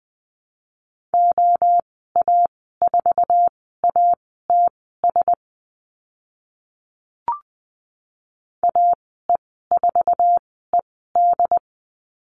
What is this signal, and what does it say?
Morse code, heard as a keyed tone, about 20 wpm, spaced as for 15 wpm, 700 Hz. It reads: OA4ATS AE4ED
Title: 082 Lesson 14g - Number 2 - Call Signs 20wpm